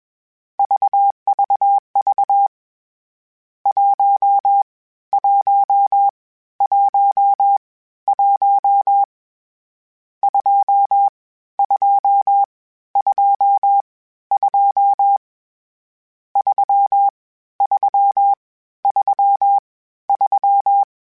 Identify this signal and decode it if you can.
Morse code, heard as a keyed tone, about 21 wpm, spaced as for 13 wpm, 800 Hz. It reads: VVV 1111 2222 3333
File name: Misc / Misc-mp3s / Numb @20